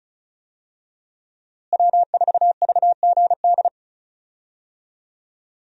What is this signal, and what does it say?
Morse code, heard as a keyed tone, about 36 wpm, 700 Hz. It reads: W4VGD
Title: USA Call Signs - 5 Characters - Repeated 3x - 35wpm